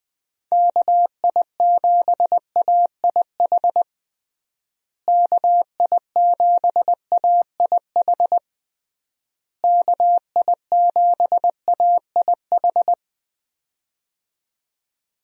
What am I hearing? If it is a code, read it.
Morse code, heard as a keyed tone, about 20 wpm, 700 Hz. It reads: KI7AIH KI7AIH KI7AIH
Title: USA Call Signs - 6 Characters - Repeated 3x - 20wpm